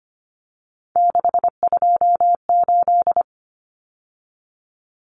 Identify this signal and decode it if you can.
Morse code, heard as a keyed tone, about 25 wpm, 700 Hz. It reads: 628
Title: Numbers - 3-Digit 25wpm